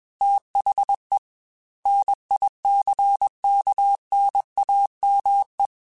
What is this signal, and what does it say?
Morse code, heard as a keyed tone, about 21 wpm, 800 Hz. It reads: THE NICKNAME
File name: Technical Subjects / Technical Subjects-mp3s / AND Gate @20